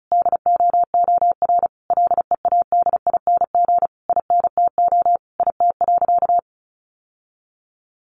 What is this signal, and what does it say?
Morse code, heard as a keyed tone, about 34 wpm, 700 Hz. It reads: DOOR LEADING INTO IT.
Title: Sentences from Top 1100 Words 35wpm